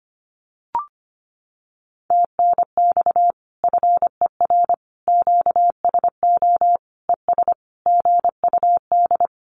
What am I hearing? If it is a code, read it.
Morse code, heard as a keyed tone, about 25 wpm, 700 Hz. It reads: TNX FER QSO ES GUD
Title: Common QSO Phrases Encore 25wpm